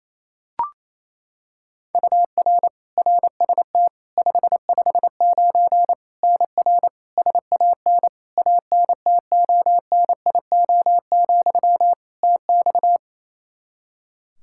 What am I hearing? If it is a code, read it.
Morse code, heard as a keyed tone, about 28 wpm, 700 Hz. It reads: UR RST 559 NR SAN ANTONIO, TX
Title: K1USN SST QSO Practice 28wpm